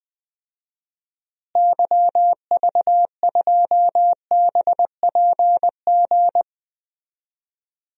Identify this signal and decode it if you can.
Morse code, heard as a keyed tone, about 20 wpm, 700 Hz. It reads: YV2BPG
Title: DXCC-100 Call Signs - 6 Characters - Repeated 2x - 20wpm